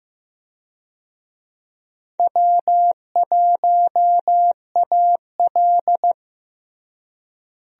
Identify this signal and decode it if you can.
Morse code, heard as a keyed tone, about 15 wpm, 700 Hz. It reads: W1AL
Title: USA Call Signs - 4 Characters - Repeated 2x - 15wpm